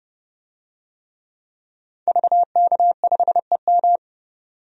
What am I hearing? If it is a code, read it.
Morse code, heard as a keyed tone, about 32 wpm, 700 Hz. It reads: VK5EM
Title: DXCC-100 Call Signs - 5 Characters - 30wpm